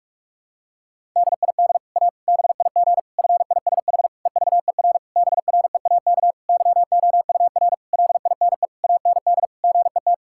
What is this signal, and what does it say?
Morse code, heard as a keyed tone, about 41 wpm, 700 Hz. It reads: DID A BIG FISH EVER BREAK YOUR LINE AND GET
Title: Sentences from Top 1000 Words - Review Part 6 - 45wpm